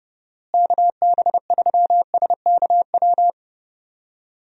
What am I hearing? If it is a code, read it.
Morse code, heard as a keyed tone, about 31 wpm, 700 Hz. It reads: KB3SKW
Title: USA Call Signs - 6 Characters - Repeated 3x - 30wpm